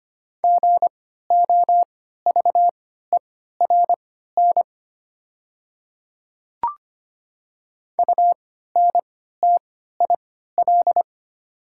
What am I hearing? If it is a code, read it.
Morse code, heard as a keyed tone, about 25 wpm, spaced as for 14 wpm, 700 Hz. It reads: GOVERN UNTIL
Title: 3x Character Spacing - Top 500 Words - 25wpm